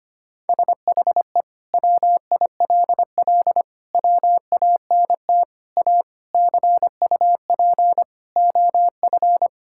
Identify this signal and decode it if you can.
Morse code, heard as a keyed tone, about 25 wpm, 700 Hz. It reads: SHE WILL WANT A CUP OF
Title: Sentences from Top 1600 Words 25wpm